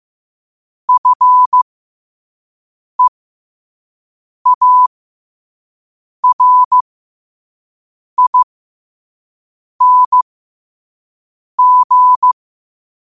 Morse code, heard as a keyed tone, about 15 wpm, spaced as for 5 wpm, 1000 Hz. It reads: FEARING
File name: 5wpm